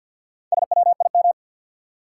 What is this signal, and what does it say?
Morse code, heard as a keyed tone, about 53 wpm, 700 Hz. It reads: SWIM
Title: Four Letter Words - 50wpm